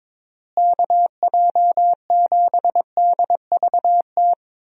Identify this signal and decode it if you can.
Morse code, heard as a keyed tone, about 21 wpm, 700 Hz. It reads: KJ7DVT
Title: USA Call Signs - 6 Characters - Repeated 3x - 22wpm